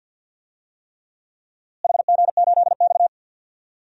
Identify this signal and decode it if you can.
Morse code, heard as a keyed tone, about 50 wpm, 700 Hz. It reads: HG8X
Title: DXCC-100 Call Signs - 4 Characters - Repeated 3x - 50wpm